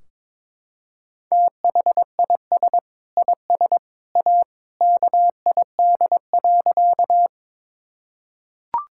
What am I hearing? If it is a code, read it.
Morse code, heard as a keyed tone, about 21 wpm, 700 Hz. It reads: THIS IS A KID.
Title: Sentences from Top 1600 Words 22wpm